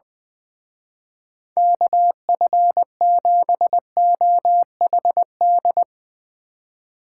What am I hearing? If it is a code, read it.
Morse code, heard as a keyed tone, about 20 wpm, 700 Hz. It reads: KF7OHD
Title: Speed Racing - USA 2x3 Call Signs 20wpm